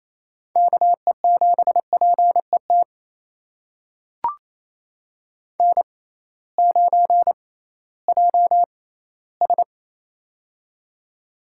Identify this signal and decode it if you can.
Morse code, heard as a keyed tone, about 28 wpm, spaced as for 16 wpm, 700 Hz. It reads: KE7PET N9JS
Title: USA Call Signs - 6x Character Spacing - 28wpm